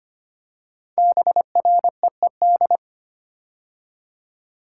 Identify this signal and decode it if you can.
Morse code, heard as a keyed tone, about 25 wpm, 700 Hz. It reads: BREED
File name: 5 Letter Words - First 3 Letters Given - 25wpm